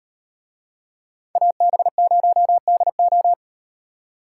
Morse code, heard as a keyed tone, about 38 wpm, 700 Hz. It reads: AB0DO